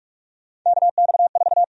Morse code, heard as a keyed tone, about 48 wpm, 700 Hz. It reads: KX4
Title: USA Call Sign Prefixes - Repeated 2x - 45wpm